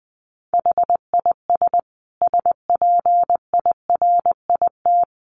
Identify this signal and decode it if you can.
Morse code, heard as a keyed tone, about 20 wpm, 700 Hz. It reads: HIS SPIRIT
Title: Sentences from Top 1400 Words 20wpm